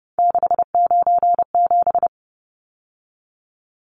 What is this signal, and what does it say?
Morse code, heard as a keyed tone, about 30 wpm, 700 Hz. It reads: 697
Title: Numbers - 3-Digit 30wpm